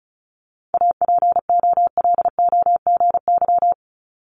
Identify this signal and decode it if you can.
Morse code, heard as a keyed tone, about 36 wpm, 690 Hz. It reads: APOLOGY
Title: Multi-letter Phonemes 35wpm